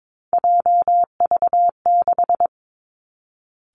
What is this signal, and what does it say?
Morse code, heard as a keyed tone, about 22 wpm, 700 Hz. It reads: JV6
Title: Three Challenging Letter-Number Combinations 22wpm